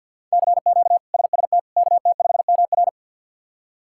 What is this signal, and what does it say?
Morse code, heard as a keyed tone, about 52 wpm, 700 Hz. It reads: CQ SST KT5MR
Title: K1USN SST QSO Practice - 50wpm